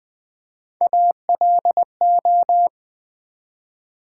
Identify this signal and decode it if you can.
Morse code, heard as a keyed tone, about 20 wpm, 700 Hz. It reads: ALO
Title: Build A Word - 1 Letter At A Time - 5-Letter Words - 20wpm